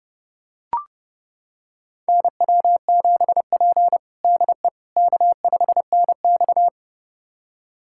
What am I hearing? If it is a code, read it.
Morse code, heard as a keyed tone, about 30 wpm, 700 Hz. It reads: NW7P DE K5NX
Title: USA Callsign DE Callsign - 4 Characters - 30wpm